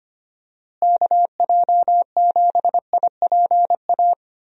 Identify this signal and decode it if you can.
Morse code, heard as a keyed tone, about 25 wpm, 700 Hz. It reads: KJ7IPA
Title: USA Call Signs - 6 Characters - Repeated 2x - 25wpm